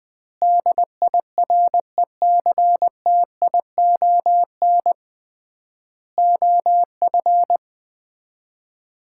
Morse code, heard as a keyed tone, about 21 wpm, 700 Hz. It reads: DIRECTION OF
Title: 3x Word-Spacing - Sets of 4 Words - 20wpm